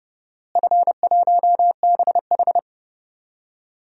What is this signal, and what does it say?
Morse code, heard as a keyed tone, about 30 wpm, 700 Hz. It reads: F1BH
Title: DXCC-100 Call Signs - 4 Characters - Repeated 2x - 30wpm